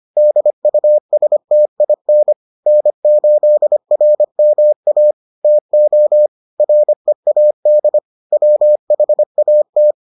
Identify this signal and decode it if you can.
Morse code, heard as a keyed tone, about 25 wpm, 600 Hz. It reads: DUSTIN N8RMA TO READ WHAT